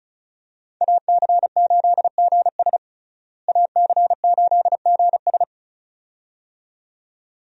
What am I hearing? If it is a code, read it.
Morse code, heard as a keyed tone, about 36 wpm, 700 Hz. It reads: AC8GS AC8GS
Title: USA Call Signs - 5 Characters - Repeated 2x - 35wpm